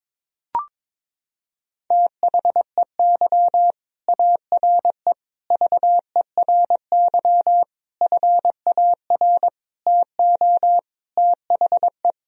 Morse code, heard as a keyed tone, about 22 wpm, 700 Hz. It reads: THEY ARE VERY FAR TO THE
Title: Sentences from Top 1000 Words - Review Part 2 - 22wpm